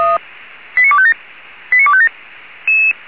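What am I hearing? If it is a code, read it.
Five-tone selective call (ZVEI), heard as quick runs of five tones, tones of about 70 ms. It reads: 89257 89257
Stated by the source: ZVEI five-tone call 89305, received by SDR; from feuerwehr-orlamuende.org fire brigade server.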